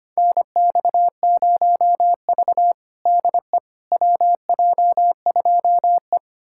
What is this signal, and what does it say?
Morse code, heard as a keyed tone, about 25 wpm, 700 Hz. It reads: NX0V DE WJ2E
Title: USA Callsign DE Callsign - 4 Characters - 25wpm